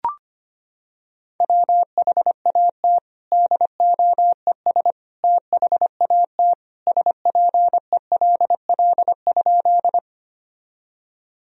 Morse code, heard as a keyed tone, about 25 wpm, 700 Hz. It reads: WHAT DOES THAT SPELL?
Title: Sentences from Top 1000 Words - Review Part 1 - 25wpm